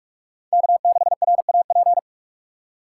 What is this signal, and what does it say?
Morse code, heard as a keyed tone, about 47 wpm, 700 Hz. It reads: K6RAP
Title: USA Call Signs - 5 Characters - 4x Character Spacing - 45wpm